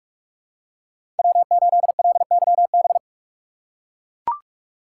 Morse code, heard as a keyed tone, about 44 wpm, 700 Hz. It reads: W8LYB
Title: USA Call Signs - 5 Characters - 6x Character Spacing - 45wpm